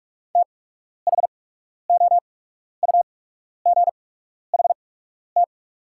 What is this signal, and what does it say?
Morse code, heard as a keyed tone, about 45 wpm, spaced as for 12 wpm, 690 Hz. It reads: THOUGHT
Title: 8x Character Spacing - Top 500 Words - 45wpm